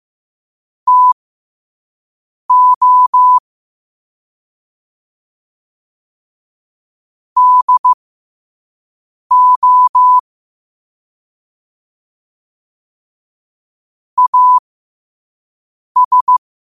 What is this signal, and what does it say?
Morse code, heard as a keyed tone, about 15 wpm, spaced as for 5 wpm, 1000 Hz. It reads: TO DO AS
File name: qotdfiles / 5wpm